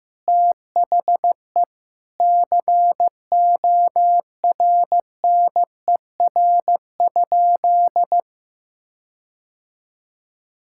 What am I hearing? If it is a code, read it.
Morse code, heard as a keyed tone, about 15 wpm, 700 Hz. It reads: THE CORNER?
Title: Sentences from Top 1600 Words 15wpm